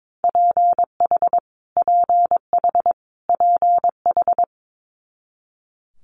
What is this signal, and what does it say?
Morse code, heard as a keyed tone, about 22 wpm, 700 Hz. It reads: PH PH PH
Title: Multi-letter Phonemes Repeated 3x 22wpm